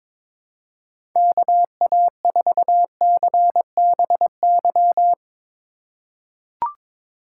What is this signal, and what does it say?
Morse code, heard as a keyed tone, about 21 wpm, 700 Hz. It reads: KA4CBY
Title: USA Call Signs - 6 Characters - 4x Character Spacing - 22wpm